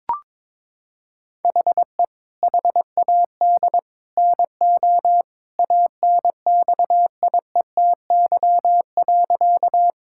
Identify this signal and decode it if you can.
Morse code, heard as a keyed tone, about 22 wpm, 700 Hz. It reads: HE HAD NO ANXIETY.
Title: Sentences from Top 1600 Words 22wpm